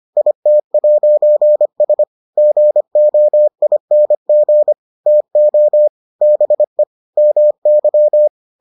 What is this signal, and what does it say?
Morse code, heard as a keyed tone, about 25 wpm, 600 Hz. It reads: IT'S GOING TO BE MY